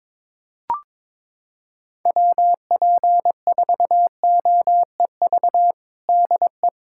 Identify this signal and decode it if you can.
Morse code, heard as a keyed tone, about 22 wpm, 700 Hz. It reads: WP4OEV DE
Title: USA Callsign DE Callsign - 6 Characters - 22wpm